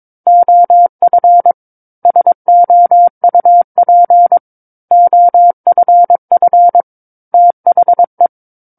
Morse code, heard as a keyed tone, about 23 wpm, 700 Hz. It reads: OF SOUP OFF THE
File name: 06-Chapter-06 22 WPM